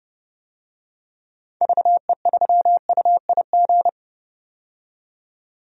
Morse code, heard as a keyed tone, about 29 wpm, 700 Hz. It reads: VE3UIG
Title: DXCC-100 Call Signs - 6 Characters - 30wpm